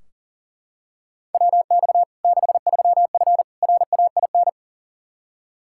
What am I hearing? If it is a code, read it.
Morse code, heard as a keyed tone, about 40 wpm, 700 Hz. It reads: WX 63F RAIN
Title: Practice QSO - Exchange 2 - Info - 40wpm